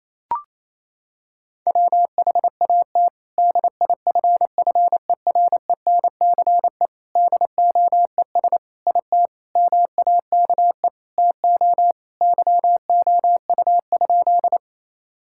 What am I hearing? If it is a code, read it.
Morse code, heard as a keyed tone, about 28 wpm, 700 Hz. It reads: WHAT DIFFERENCE DOES IT MAKE TO YOU?
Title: Sentences from Top 1000 Words - Review Part 5 - 28wpm